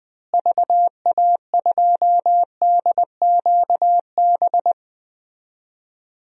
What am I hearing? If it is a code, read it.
Morse code, heard as a keyed tone, about 20 wpm, 700 Hz. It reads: VA2DQB
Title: DXCC-100 Call Signs - 6 Characters - Repeated 3x - 20wpm